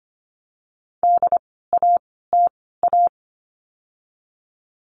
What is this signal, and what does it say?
Morse code, heard as a keyed tone, about 25 wpm, spaced as for 16 wpm, 700 Hz. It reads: DATA
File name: course / 25:10wpm / 094 Lesson 16f - Letter c - Words 25wpm